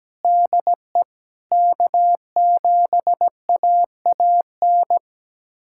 Morse code, heard as a keyed tone, about 17 wpm, 700 Hz. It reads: DE K7AAN
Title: USA Callsign DE Callsign - 5 Characters - 17wpm